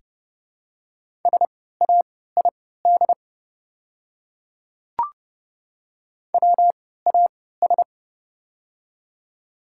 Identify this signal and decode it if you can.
Morse code, heard as a keyed tone, about 30 wpm, spaced as for 17 wpm, 700 Hz. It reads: SAID WAS
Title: 3x Character Spacing - Top 100 Words - 30wpm